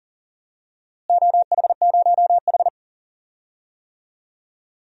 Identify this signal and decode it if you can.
Morse code, heard as a keyed tone, about 40 wpm, 700 Hz. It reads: OH0H
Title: DXCC-100 Call Signs - 4 Characters - Repeated 2x - 40wpm